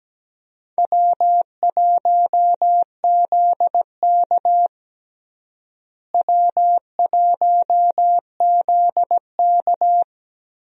Morse code, heard as a keyed tone, about 17 wpm, 700 Hz. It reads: W1ZK W1ZK
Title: USA Call Signs - 4 Characters - Repeated 3x - 17wpm